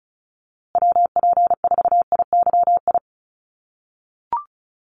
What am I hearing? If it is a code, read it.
Morse code, heard as a keyed tone, about 34 wpm, 700 Hz. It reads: WP4IYI
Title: USA Call Signs Repeated 3x 35wpm